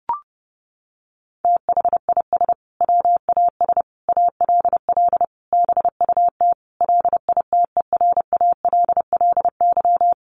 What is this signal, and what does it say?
Morse code, heard as a keyed tone, about 28 wpm, 700 Hz. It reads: THIS WAS ALL BUT LITERALLY
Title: Sentences from Top 1400 Words 30wpm